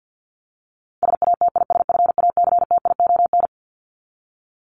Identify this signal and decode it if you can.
Morse code, heard as a keyed tone, about 48 wpm, 700 Hz. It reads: SATISFACTION
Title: Top 1751-2000 Words 50wpm